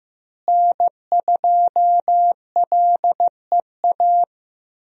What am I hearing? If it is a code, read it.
Morse code, heard as a keyed tone, about 15 wpm, 700 Hz. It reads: N2LEA
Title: USA Call Signs - 5 Characters - Repeated 2x - 15wpm